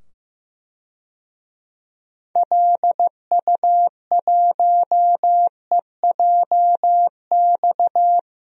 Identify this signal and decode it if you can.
Morse code, heard as a keyed tone, about 15 wpm, 700 Hz. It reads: LU1EJX
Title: DXCC-100 Call Signs - 6 Characters - 15wpm